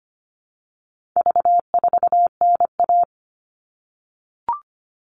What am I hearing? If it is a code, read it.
Morse code, heard as a keyed tone, about 25 wpm, 700 Hz. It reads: V4NA
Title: All Cut Numbers - 4-Digit 25wpm